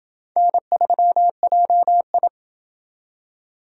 Morse code, heard as a keyed tone, about 26 wpm, 700 Hz. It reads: N3JI